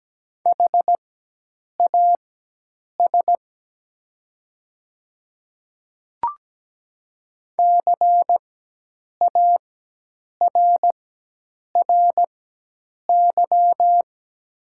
Morse code, heard as a keyed tone, about 17 wpm, spaced as for 8 wpm, 700 Hz. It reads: HAS CARRY
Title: 4x Character Spacing - Top 300 Words - 17wpm